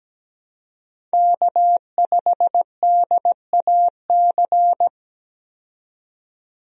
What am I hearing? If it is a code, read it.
Morse code, heard as a keyed tone, about 17 wpm, 700 Hz. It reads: K5DAC
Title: USA Call Signs - 5 Characters - 17wpm